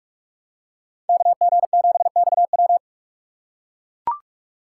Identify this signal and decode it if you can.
Morse code, heard as a keyed tone, about 43 wpm, 700 Hz. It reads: KG7XW
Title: USA Call Signs - 5 Characters - 2x Character Spacing - 45wpm